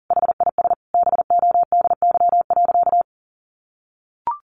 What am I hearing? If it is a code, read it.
Morse code, heard as a keyed tone, about 40 wpm, 700 Hz. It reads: HIS BODY.
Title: A Sound Of Thunder 40wpm